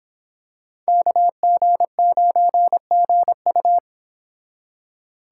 Morse code, heard as a keyed tone, about 26 wpm, 700 Hz. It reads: KG9GU